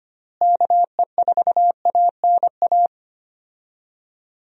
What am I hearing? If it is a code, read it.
Morse code, heard as a keyed tone, about 25 wpm, 700 Hz. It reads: KE4ANA
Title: USA Call Signs - 6 Characters - Repeated 3x - 25wpm